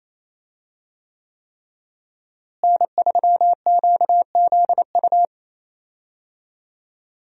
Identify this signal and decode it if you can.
Morse code, heard as a keyed tone, about 28 wpm, 700 Hz. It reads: N3QZU
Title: USA Call Signs - 5 Characters - Repeated 3x - 28wpm